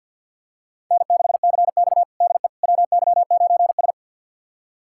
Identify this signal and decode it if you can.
Morse code, heard as a keyed tone, about 53 wpm, 700 Hz. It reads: N6CX DE WY9S
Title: USA Callsign DE Callsign - 4 Characters - 50wpm